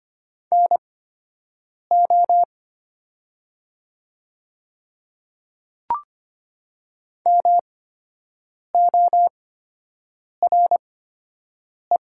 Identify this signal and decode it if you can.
Morse code, heard as a keyed tone, about 25 wpm, spaced as for 7 wpm, 700 Hz. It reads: NO MORE
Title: 8x Character Spacing - Top 500 Words - 25wpm